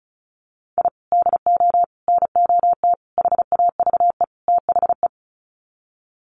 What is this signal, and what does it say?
Morse code, heard as a keyed tone, about 34 wpm, 700 Hz. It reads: I DO NOT HAVE THE
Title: Sets of 5 Words 35wpm